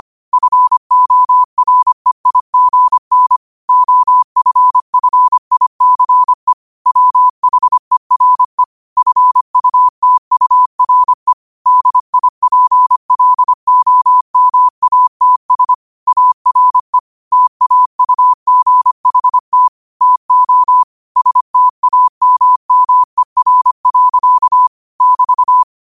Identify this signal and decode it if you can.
Morse code, heard as a keyed tone, about 25 wpm, 1000 Hz. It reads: FOREIGN OFFICE WHERE FUTURE DIPLOMATS ARE TAUGHT TO STAMMER. <BT>
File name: qotdfiles / 25wpm